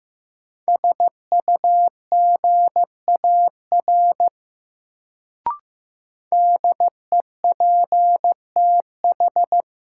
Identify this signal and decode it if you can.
Morse code, heard as a keyed tone, about 15 wpm, 700 Hz. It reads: SUGAR DEPTH
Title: Sentences from Top 1200 Words - New Word Drill Alt - 15wpm